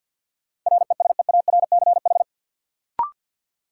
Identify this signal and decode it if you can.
Morse code, heard as a keyed tone, about 53 wpm, 700 Hz. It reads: RESEARCH
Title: Sentences from Top 1400 Words - New Word Drill Alt - 50wpm